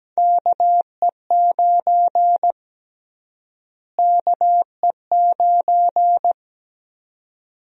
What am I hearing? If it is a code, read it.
Morse code, heard as a keyed tone, about 17 wpm, 700 Hz. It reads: KE9 KE9
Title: USA Call Sign Prefixes - Repeated 2x - 17wpm